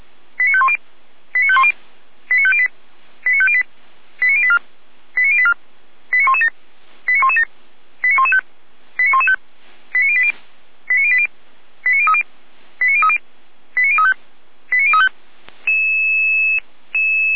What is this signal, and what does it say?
Five-tone selective call (ZVEI), heard as quick runs of five tones, tones of about 70 ms. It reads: 89510 89510 89508 89508 89084 89084 89117 89117 89115 89115 89080 89080 89030 89030 89035 89035